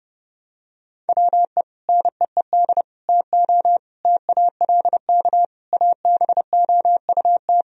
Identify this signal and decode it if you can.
Morse code, heard as a keyed tone, about 30 wpm, 700 Hz. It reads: WE NEED TO TALK ABOUT
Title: Sentences from Top 1000 Words - Review Part 4 - 30wpm